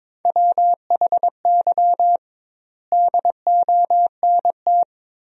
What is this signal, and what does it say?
Morse code, heard as a keyed tone, about 22 wpm, 700 Hz. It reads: WHY DONT
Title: 2x Word-Spacing - Sentences from Top 300 Words - 22wpm